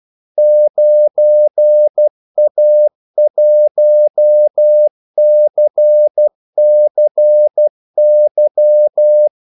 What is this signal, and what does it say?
Morse code, heard as a keyed tone, about 12 wpm, 600 Hz. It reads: 9A1CCY
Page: 2 ekstra ordmellemrum mellem kaldesignalerne